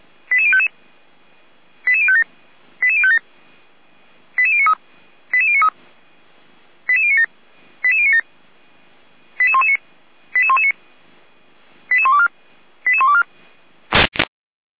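Five-tone selective call (ZVEI), heard as quick runs of five tones, tones of about 70 ms. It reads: 88055 88057 88057 88093 88093 88097 88097 88119 88119 88124 88124
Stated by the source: ZVEI five-tone call 88053 received on SDR, from the feuerwehr-orlamuende.org server, a fire brigade.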